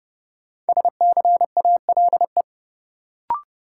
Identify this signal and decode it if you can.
Morse code, heard as a keyed tone, about 30 wpm, 700 Hz. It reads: SCALE
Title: Sentences from Top 1000 Words - Review Part 4 - New Word Drill - 30wpm